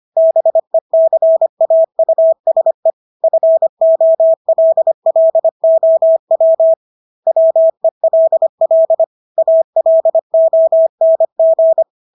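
Morse code, heard as a keyed tone, about 25 wpm, 650 Hz. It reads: BECAUSE FOLLOW WELL ALONG
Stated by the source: Morse practice of English words at 25 wpm from CW med Gnister.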